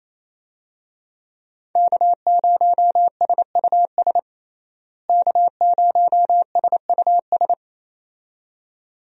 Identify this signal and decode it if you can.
Morse code, heard as a keyed tone, about 28 wpm, 700 Hz. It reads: K0SUS K0SUS
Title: USA Call Signs - 5 Characters - Repeated 2x - 28wpm